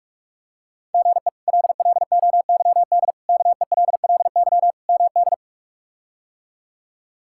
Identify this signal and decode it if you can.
Morse code, heard as a keyed tone, about 47 wpm, 700 Hz. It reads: GE LLOYD KELLY MD
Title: K1USN SST QSO Practice - 45wpm